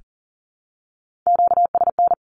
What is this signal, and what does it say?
Morse code, heard as a keyed tone, about 40 wpm, 700 Hz. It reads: QSN
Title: Q-Codes 40wpm